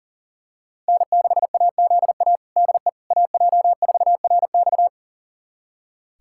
Morse code, heard as a keyed tone, about 40 wpm, 700 Hz. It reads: N6AZA DE AJ4RX
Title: USA Callsign DE Callsign - 5 Characters - 40wpm